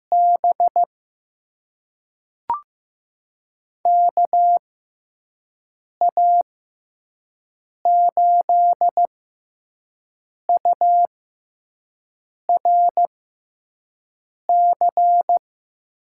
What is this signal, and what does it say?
Morse code, heard as a keyed tone, about 15 wpm, spaced as for 5 wpm, 700 Hz. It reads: B KA8URC
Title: USA Call Signs - 6 Characters - 6x Character Spacing - 15wpm